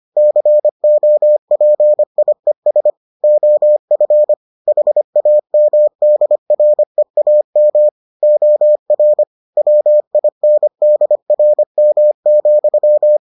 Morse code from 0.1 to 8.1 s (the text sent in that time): COPIES OF HAMDREAM